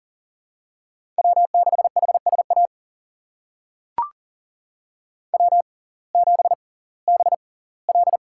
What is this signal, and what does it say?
Morse code, heard as a keyed tone, about 40 wpm, spaced as for 22 wpm, 700 Hz. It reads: W6HSA W7BL
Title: USA Call Signs - 5 Characters - 6x Character Spacing - 40wpm